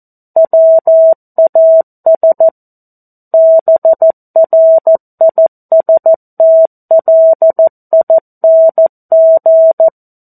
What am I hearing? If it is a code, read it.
Morse code, heard as a keyed tone, about 14 wpm, 650 Hz. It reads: WAS BRISTLING
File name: 02-Chapter-02 14 WPM